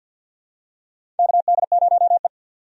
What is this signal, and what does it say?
Morse code, heard as a keyed tone, about 52 wpm, 700 Hz. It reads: KD0E
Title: USA Call Signs - 4 Characters - 50wpm